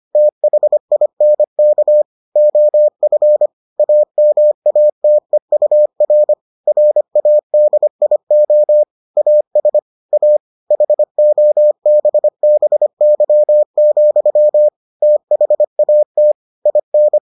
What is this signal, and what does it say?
Morse code, heard as a keyed tone, about 25 wpm, 600 Hz. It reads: THINK OF AMATEUR RADIO AS A HOBBY, THAT IN